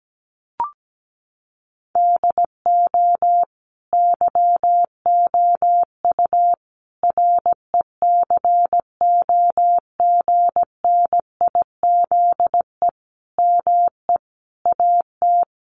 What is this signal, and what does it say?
Morse code, heard as a keyed tone, about 17 wpm, 700 Hz. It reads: DO YOU RECOGNIZE ME AT
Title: Sentences from Top 1400 Words 17wpm